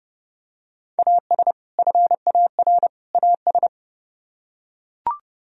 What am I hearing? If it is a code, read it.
Morse code, heard as a keyed tone, about 30 wpm, 700 Hz. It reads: AS FAR AS